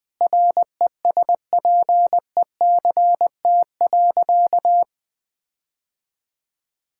Morse code, heard as a keyed tone, about 20 wpm, 700 Hz. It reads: RESPECT.
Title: Sentences with 4-Word Phrases - 20wpm